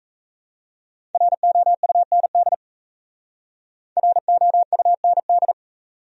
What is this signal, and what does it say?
Morse code, heard as a keyed tone, about 34 wpm, 700 Hz. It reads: ROUND ROUND
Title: Speed-Racing Top 300 Words 28wpm